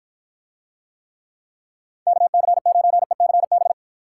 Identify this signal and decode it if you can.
Morse code, heard as a keyed tone, about 52 wpm, 700 Hz. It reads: KC9ECB